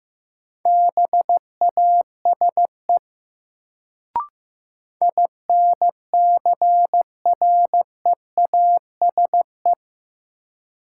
Morse code, heard as a keyed tone, about 16 wpm, 700 Hz. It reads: BASE INCREASE
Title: Sets of 5 Words - New Word Drill - 15wpm